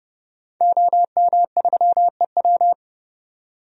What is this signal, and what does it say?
Morse code, heard as a keyed tone, about 29 wpm, 700 Hz. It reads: OM3EW
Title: DXCC-100 Call Signs - 5 Characters - 30wpm